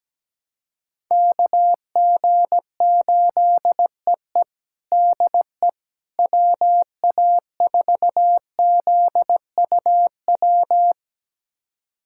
Morse code, heard as a keyed tone, about 17 wpm, 700 Hz. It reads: KG8EE DE WA4ZUW
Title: USA Callsign DE Callsign 17wpm